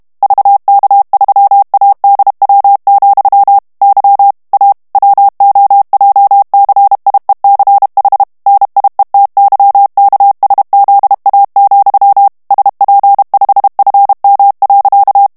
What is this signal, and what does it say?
Morse code, heard as a keyed tone, about 32 wpm, 800 Hz. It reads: VK3ADW, Y A WOJCIECH NIETYKSZA, SP5FM.
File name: morse mp3 / ESPAGNOL TEXTE EN CLAIR 1 VITESSE 30 MOTS MINUTES